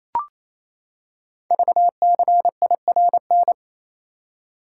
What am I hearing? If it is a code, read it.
Morse code, heard as a keyed tone, about 27 wpm, 690 Hz. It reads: VCIRN
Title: Code Groups - 5 Letters - 28wpm